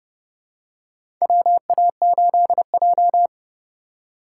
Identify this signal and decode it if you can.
Morse code, heard as a keyed tone, about 30 wpm, 700 Hz. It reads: WA8J
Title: USA Call Signs - 4x Character Spacing - 30wpm